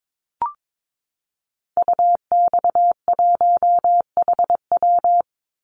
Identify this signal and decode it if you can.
Morse code, heard as a keyed tone, about 22 wpm, 700 Hz. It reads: UX1HW
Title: Call Signs Repeated 3x - 22wpm